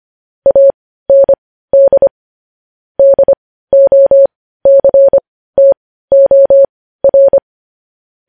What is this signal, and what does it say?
Morse code, heard as a keyed tone, about 25 wpm, spaced as for 15 wpm, 550 Hz. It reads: AND DOCTOR
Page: news headlines in morse code at 15 wpm